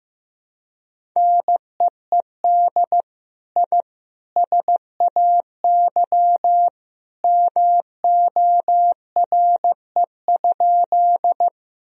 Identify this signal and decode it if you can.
Morse code, heard as a keyed tone, about 16 wpm, 700 Hz. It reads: NEED I SAY MORE?
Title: Sentences from Top 1000 Words - Review Part 1 - 15wpm